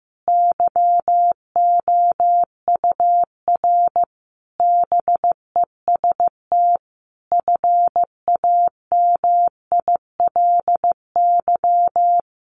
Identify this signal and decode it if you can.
Morse code, heard as a keyed tone, about 15 wpm, 700 Hz. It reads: YOUR BEST FAMILY
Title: Sentences from Top 1200 Words 15wpm